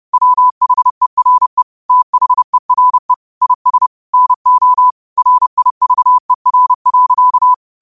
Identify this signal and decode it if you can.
Morse code, heard as a keyed tone, about 30 wpm, 1000 Hz. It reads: WHERE THERE IS NO RIVER.
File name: qotdfiles / 30wpm